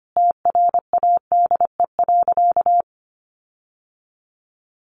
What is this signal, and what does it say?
Morse code, heard as a keyed tone, about 25 wpm, 700 Hz. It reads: TRADE.
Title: Sentences from Top 1300 Words 25wpm